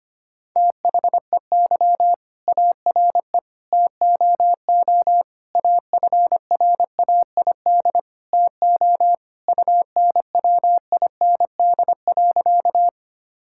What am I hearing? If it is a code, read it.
Morse code, heard as a keyed tone, about 25 wpm, 700 Hz. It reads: THEY ARE TOO AFRAID TO UNWIND.
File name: practice / 25wpm / All-Age Appropriate Jokes - 25wpm